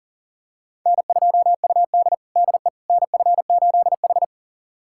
Morse code, heard as a keyed tone, about 40 wpm, 700 Hz. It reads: N2UD DE NF8H
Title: USA Callsign DE Callsign - 4 Characters - 40wpm